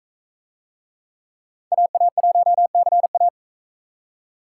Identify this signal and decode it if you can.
Morse code, heard as a keyed tone, about 44 wpm, 700 Hz. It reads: AA1CA